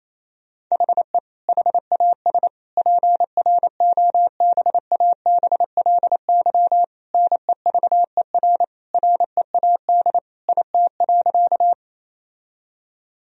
Morse code, heard as a keyed tone, about 28 wpm, 700 Hz. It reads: HE HAS PROBABLY NEVER READ IT.
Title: Sentences from Top 1000 Words - Review Part 4 - 28wpm